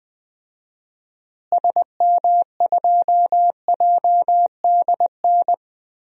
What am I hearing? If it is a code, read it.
Morse code, heard as a keyed tone, about 20 wpm, 700 Hz. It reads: SM2JDN
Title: DXCC-100 Call Signs - 6 Characters - Repeated 2x - 20wpm